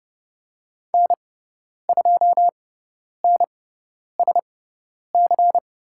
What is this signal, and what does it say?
Morse code, heard as a keyed tone, about 30 wpm, spaced as for 10 wpm, 700 Hz. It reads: N2NSC
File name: Lesson 17g - Letter u - Call Signs 30wpm